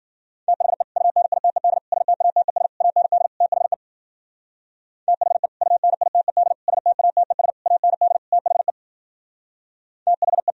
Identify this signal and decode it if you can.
Morse code, heard as a keyed tone, about 52 wpm, 700 Hz. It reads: THE UNITED STATES AND THE THE UNITED STATES AND THE THE